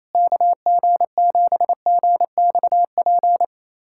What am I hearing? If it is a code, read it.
Morse code, heard as a keyed tone, about 28 wpm, 700 Hz. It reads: KG7GXP
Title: USA Callsign DE Callsign - 6 Characters - 28wpm